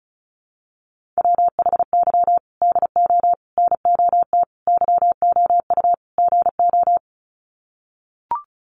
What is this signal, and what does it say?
Morse code, heard as a keyed tone, about 34 wpm, 700 Hz. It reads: WHY DO NOT YOU GO
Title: Sets of 5 Words Encore 35wpm